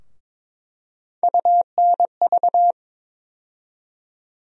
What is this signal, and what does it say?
Morse code, heard as a keyed tone, about 22 wpm, 700 Hz. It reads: UNV
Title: USA Call Sign Postfixes - 3 Letters 22wpm